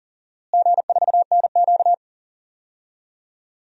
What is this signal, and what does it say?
Morse code, heard as a keyed tone, about 40 wpm, 700 Hz. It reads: G4NQ
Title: DXCC-100 Call Signs - 4 Characters - Repeated 2x - 40wpm